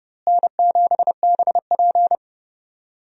Morse code, heard as a keyed tone, about 30 wpm, 700 Hz. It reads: N7BP
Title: USA Call Signs - 4 Characters - Repeated 2x - 30wpm